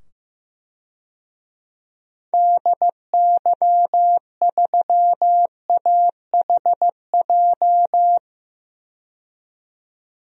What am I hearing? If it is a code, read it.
Morse code, heard as a keyed tone, about 15 wpm, 700 Hz. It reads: DY3AHJ
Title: DXCC-100 Call Signs - 6 Characters - 15wpm